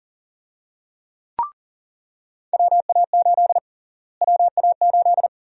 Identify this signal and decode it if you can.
Morse code, heard as a keyed tone, about 37 wpm, 700 Hz. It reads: WA8 WA8
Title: USA Call Sign Prefixes - Repeated 2x - 40wpm